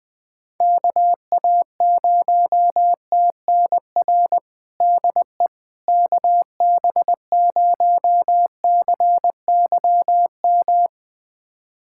Morse code, heard as a keyed tone, about 20 wpm, 700 Hz. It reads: KA0TNR DE KB0CYM
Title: USA Callsign DE Callsign 20wpm